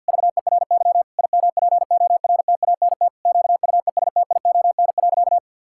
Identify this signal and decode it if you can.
Morse code, heard as a keyed tone, about 50 wpm, 700 Hz. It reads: VERY IMPORTANT QUESTION.
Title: Sentences from Top 1000 Words - Review Part 6 - 50wpm